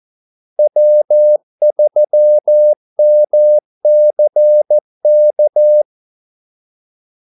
Morse code, heard as a keyed tone, about 14 wpm, 600 Hz. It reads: W3MCK
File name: Callsigns-14wpm